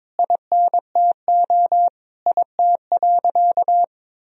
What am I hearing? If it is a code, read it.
Morse code, heard as a keyed tone, about 22 wpm, 700 Hz. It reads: INTO IT.
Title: Sentences from Top 1000 Words - Review Part 3 - 22wpm